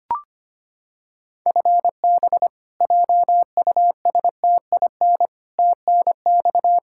Morse code, heard as a keyed tone, about 25 wpm, 700 Hz. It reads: FB JUSTIN TNX
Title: Practice QSO - Exchange 3 - More Info - 25wpm